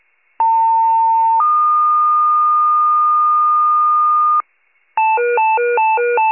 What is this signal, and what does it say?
Two-tone page, heard as a one-second tone, then a three-second tone, about 900 then 1220 Hz. It is heard from 0.4 to 4.4 s.
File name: pager / 94 RESCUE OFC / wcalert